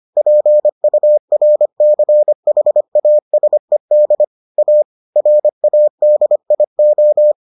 Morse code, heard as a keyed tone, about 25 wpm, 600 Hz. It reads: PURCHASED A RADIO